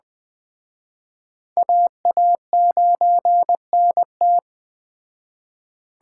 Morse code, heard as a keyed tone, about 21 wpm, 700 Hz. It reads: AA9NT
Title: Speed Racing - USA 2x2 Call Signs 17wpm